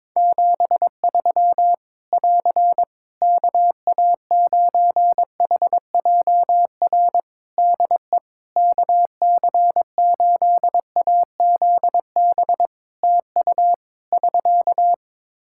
Morse code, heard as a keyed tone, about 22 wpm, 700 Hz. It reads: 73 <AR> KA9HJR DE KC8AZB TU <SK>
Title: Practice QSO - Exchange 4 - Ending - 22wpm